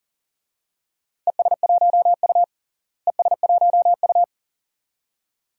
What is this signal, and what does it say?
Morse code, heard as a keyed tone, about 40 wpm, 700 Hz. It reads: ES1U ES1U
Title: DXCC-100 Call Signs - 4 Characters - Repeated 2x - 40wpm